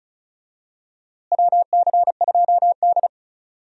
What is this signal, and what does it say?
Morse code, heard as a keyed tone, about 34 wpm, 700 Hz. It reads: WC2D
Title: USA Call Signs - 4 Characters - Repeated 3x - 35wpm